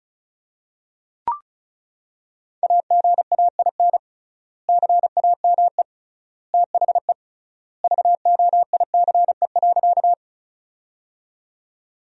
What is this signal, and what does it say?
Morse code, heard as a keyed tone, about 36 wpm, 700 Hz. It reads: AGAIN CAME THE VOICE.
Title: 3x Word-Spacing - Sentences from Top 500 Words - 35wpm